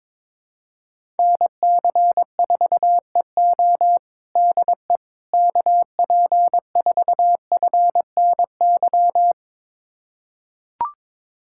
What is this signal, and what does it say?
Morse code, heard as a keyed tone, about 22 wpm, 700 Hz. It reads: NC4EO DE KP4FNY
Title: USA Callsign DE Callsign 22wpm